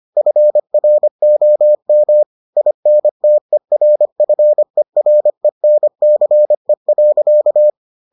Morse code, heard as a keyed tone, about 25 wpm, 600 Hz. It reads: FROM INTERFERENCE.